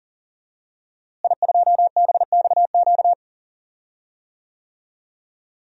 Morse code, heard as a keyed tone, about 40 wpm, 700 Hz. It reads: I2BXQ
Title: DXCC-100 Call Signs - 5 Characters - Repeated 3x - 40wpm